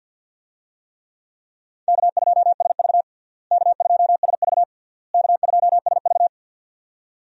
Morse code, heard as a keyed tone, about 51 wpm, 700 Hz. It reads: K2SV K2SV K2SV
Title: USA Call Signs - 4 Characters - Repeated 3x - 50wpm